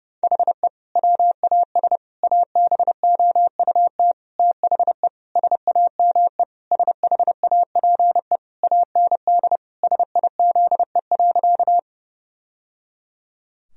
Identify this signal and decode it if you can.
Morse code, heard as a keyed tone, about 30 wpm, 700 Hz. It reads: HE WAS ABOUT THE SAME SHAPE AND SIZE.
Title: Sentences from Top 1000 Words - Review Part 3 - 30wpm